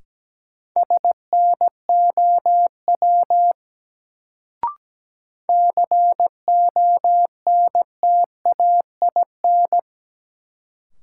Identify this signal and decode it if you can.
Morse code, heard as a keyed tone, about 17 wpm, 700 Hz. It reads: SNOW CONTAIN
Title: Sentences from Top 500 Words - New Word Drill - 17wpm